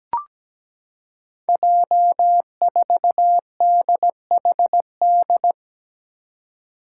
Morse code, heard as a keyed tone, about 17 wpm, 700 Hz. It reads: J4DHD
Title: Five Challenging Letter-Number Combinations 17wpm